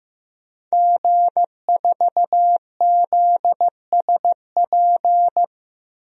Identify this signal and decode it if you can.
Morse code, heard as a keyed tone, about 16 wpm, 700 Hz. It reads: G4ZSP
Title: DXCC-100 Call Signs - 5 Characters - 15wpm